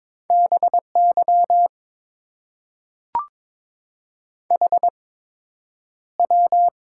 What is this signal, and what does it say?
Morse code, heard as a keyed tone, about 23 wpm, spaced as for 10 wpm, 700 Hz. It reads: BY HW
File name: Top 90 QSO Elements - 8x Character Spacing - 22wpm